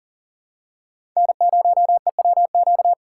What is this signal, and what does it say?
Morse code, heard as a keyed tone, about 39 wpm, 700 Hz. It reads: N0EWQ
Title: USA Call Signs - 5 Characters - 4x Character Spacing - 40wpm